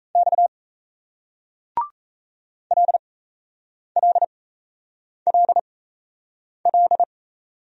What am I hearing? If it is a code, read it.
Morse code, heard as a keyed tone, about 38 wpm, spaced as for 8 wpm, 700 Hz. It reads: X LLLL